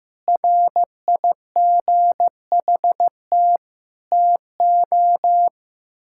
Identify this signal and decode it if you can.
Morse code, heard as a keyed tone, about 16 wpm, 700 Hz. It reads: RIGHT TO
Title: Sentences from Top 1000 Words - Review Part 6 - 15wpm